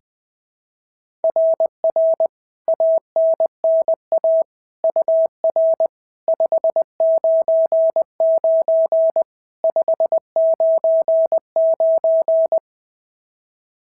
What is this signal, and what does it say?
Morse code, heard as a keyed tone, about 20 wpm, 690 Hz. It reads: RR ANNA UR 599 599
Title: SOTA QSO Practice 20wpm